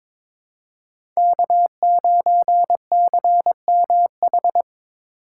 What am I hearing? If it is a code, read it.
Morse code, heard as a keyed tone, about 22 wpm, 700 Hz. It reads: K9CMH
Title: USA Call Signs - 5 Characters - 3x Character Spacing - 22wpm